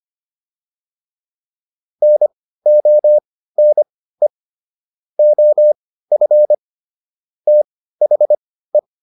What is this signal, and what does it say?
Morse code, heard as a keyed tone, about 25 wpm, spaced as for 15 wpm, 600 Hz. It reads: NONE OF THE